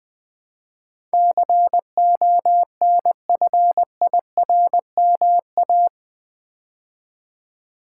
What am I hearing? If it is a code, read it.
Morse code, heard as a keyed tone, about 20 wpm, 700 Hz. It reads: CONFIRMA
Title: Build A Word - 4 Letters At A Time - 12-Letter Words - 20wpm